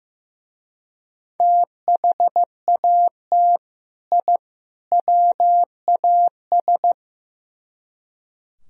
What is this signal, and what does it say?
Morse code, heard as a keyed tone, about 17 wpm, 700 Hz. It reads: THAT I WAS
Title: Speed-Racing - Sets of 3 Words 15wpm